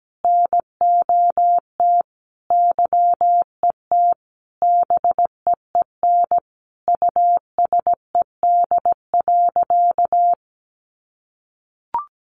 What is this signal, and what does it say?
Morse code, heard as a keyed tone, about 17 wpm, 700 Hz. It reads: NOT YET BEEN USED.
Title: Sentences from Top 700 Words 17wpm